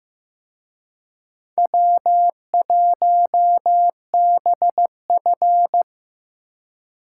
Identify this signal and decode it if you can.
Morse code, heard as a keyed tone, about 18 wpm, 700 Hz. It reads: W1BF
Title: Speed Racing - USA 1x2 Call Signs 15wpm